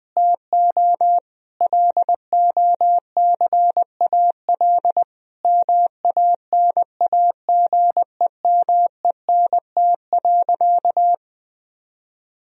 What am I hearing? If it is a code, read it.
Morse code, heard as a keyed tone, about 20 wpm, 700 Hz. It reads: TO LOCAL MANAGEMENT.
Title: Sentences from Top 1000 Words - Review Part 4 - 20wpm